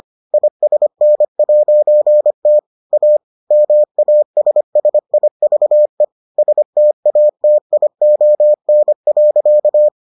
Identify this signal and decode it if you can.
Morse code, heard as a keyed tone, about 25 wpm, 600 Hz. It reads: ISN'T A MASSIVE STATION.